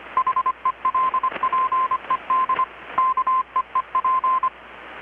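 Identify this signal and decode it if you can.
Morse code, heard as a keyed tone, about 25 wpm, 1040 Hz. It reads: HELPED KEEP
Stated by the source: Morse from KPH's Night of Nights XX maritime radiotelegraphy event, received in Chiapas, Mexico about 3,800 km away on a Kenwood R-600 with a grounded 10 m long-wire antenna.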